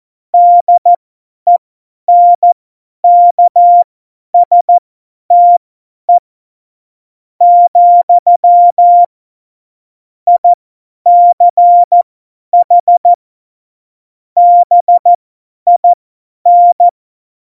Morse code, heard as a keyed tone, about 14 wpm, spaced as for 10 wpm, 700 Hz. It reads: DENKSTE , ICH BIN
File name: Morsekurs / Witze / Witz 24 April 2018, 50 F70